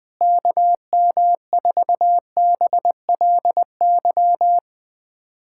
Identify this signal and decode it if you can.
Morse code, heard as a keyed tone, about 20 wpm, 700 Hz. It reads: KM4BLY
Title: USA Callsign DE Callsign 20wpm